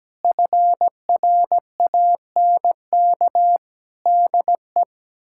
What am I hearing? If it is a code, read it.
Morse code, heard as a keyed tone, about 17 wpm, 700 Hz. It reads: FRANK DE
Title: K1USN SST QSO Practice - 17wpm